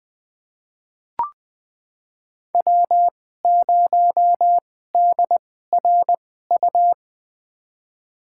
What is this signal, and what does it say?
Morse code, heard as a keyed tone, about 20 wpm, spaced as for 15 wpm, 700 Hz. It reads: W0DRU
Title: USA Call Signs - 5 Characters - 2x Character Spacing - 20wpm